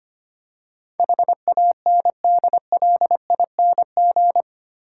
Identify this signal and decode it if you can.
Morse code, heard as a keyed tone, about 25 wpm, 700 Hz. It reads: HANDLING
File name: 8 Letter Words - First 3 Letters Given - 25wpm